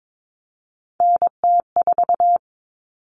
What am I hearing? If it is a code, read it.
Morse code, heard as a keyed tone, about 22 wpm, 700 Hz. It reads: NT4
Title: All Cut Numbers - 3-Digit 22wpm